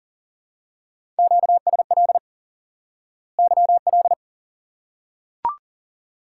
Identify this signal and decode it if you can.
Morse code, heard as a keyed tone, about 40 wpm, 700 Hz. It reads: QSL YL
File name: course / instant-qso-element-course-3 / IQE Course 3 - Lesson 89 - YAGI - Two Elements - 40wpm